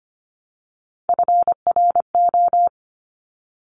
Morse code, heard as a keyed tone, about 25 wpm, 700 Hz. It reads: FRO
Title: Top Three Letter Combinations in Words 25wpm